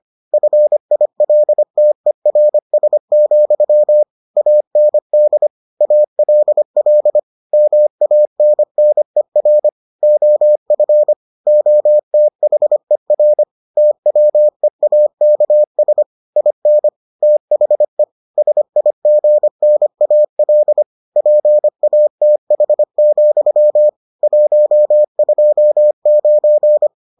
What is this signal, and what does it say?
Morse code, heard as a keyed tone, about 25 wpm, 600 Hz. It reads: FILTERS, AND ALL MANNER OF OTHER TWEAKS IN THE SIGNAL PATH, 129